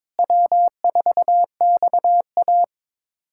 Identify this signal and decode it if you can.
Morse code, heard as a keyed tone, about 21 wpm, 700 Hz. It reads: W4XA
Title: USA Callsign DE Callsign - 4 Characters - 22wpm